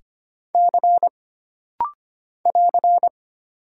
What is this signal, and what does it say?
Morse code, heard as a keyed tone, about 25 wpm, 700 Hz. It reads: C <AR>
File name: Single Character (Letter:Number:Punctuation:Prosign) - Rapid-Fire - 25wpm